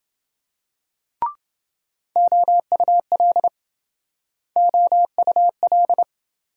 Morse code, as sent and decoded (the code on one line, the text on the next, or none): --- ..- .-.. / --- ..- .-..
OUL OUL